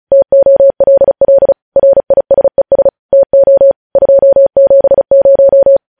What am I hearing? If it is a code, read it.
Morse code, heard as a keyed tone, about 36 wpm, 550 Hz. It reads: TOLL RISES TO 270
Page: news headlines in morse code at 35 wpm